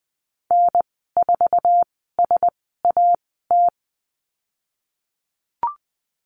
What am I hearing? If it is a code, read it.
Morse code, heard as a keyed tone, about 20 wpm, 700 Hz. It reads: N 4 S A T
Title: 048 Lesson 9g - Number 4 - Call Signs 20wpm